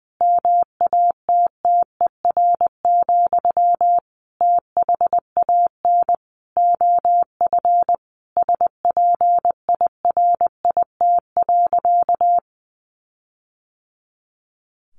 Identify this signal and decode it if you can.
Morse code, heard as a keyed tone, about 19 wpm, 700 Hz. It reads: MATTER, THAN OF SPIRIT.
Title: Sentences from Top 1400 Words 20wpm